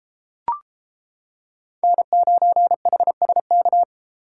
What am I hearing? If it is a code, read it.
Morse code, heard as a keyed tone, about 30 wpm, 700 Hz. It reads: N9HSK